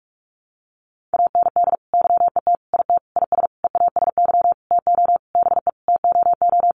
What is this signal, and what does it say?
Morse code, heard as a keyed tone, about 44 wpm, 700 Hz. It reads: AND YET IT IS EASY TO BE TOO
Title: Sentences from Top 1400 Words 45wpm